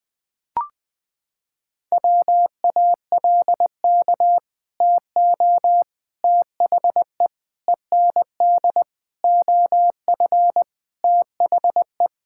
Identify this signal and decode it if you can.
Morse code, heard as a keyed tone, about 20 wpm, 700 Hz. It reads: WALK TO THE END OF THE
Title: Sentences with 4-Word Phrases - 20wpm